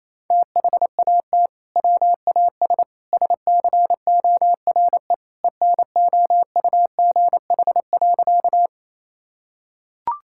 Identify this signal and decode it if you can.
Morse code, heard as a keyed tone, about 27 wpm, 700 Hz. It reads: THAT WAS SCORE ENOUGH.
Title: Sentences from Top 1600 Words 28wpm